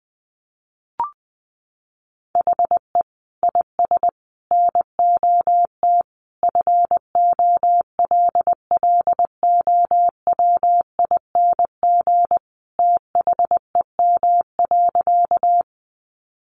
Morse code, heard as a keyed tone, about 20 wpm, 700 Hz. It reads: HE IS NOT FOLLOWING THEM.
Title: Sentences from Top 700 Words 20wpm